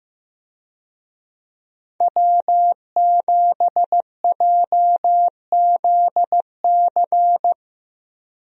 Morse code, heard as a keyed tone, about 15 wpm, 700 Hz. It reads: W7JZC
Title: USA Call Signs - 5 Characters - Repeated 2x - 15wpm